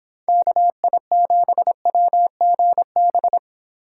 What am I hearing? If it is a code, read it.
Morse code, heard as a keyed tone, about 26 wpm, 700 Hz. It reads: KI7WGB